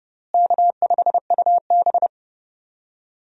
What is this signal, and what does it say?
Morse code, heard as a keyed tone, about 29 wpm, 700 Hz. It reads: K5UB
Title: USA Call Signs - 4 Characters - Repeated 2x - 30wpm